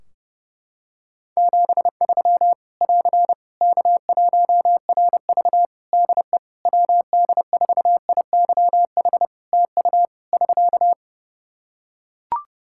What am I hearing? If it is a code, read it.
Morse code, heard as a keyed tone, about 32 wpm, 700 Hz. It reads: 73 <AR> K1RV DE WD4IYH TU <SK>
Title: Practice QSO - Exchange 4 - Ending - 30wpm